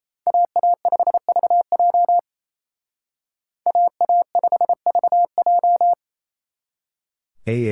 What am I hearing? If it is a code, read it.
Morse code, heard as a keyed tone, about 32 wpm, 700 Hz. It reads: AA5VJ AA5VJ
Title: Speed Racing - USA 2x2 Call Signs 28wpm